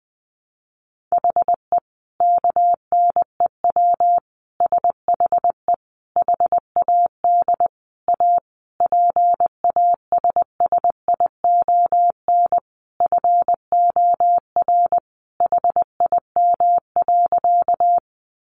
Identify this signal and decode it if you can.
Morse code, heard as a keyed tone, about 20 wpm, 700 Hz. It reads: HE KNEW SHE HAD A PASSION FOR HIM.
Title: Sentences from Top 1300 Words 20wpm